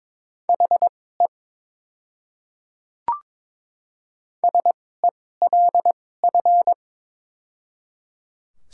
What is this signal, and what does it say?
Morse code, heard as a keyed tone, about 22 wpm, spaced as for 16 wpm, 700 Hz. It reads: HE SELF
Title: 2x Character Spacing - Top 200 Words - 22wpm